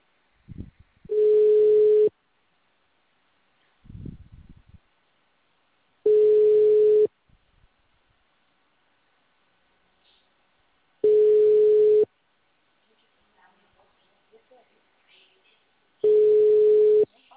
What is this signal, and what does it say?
Ringback tone (Europe, 425 Hz), with speech.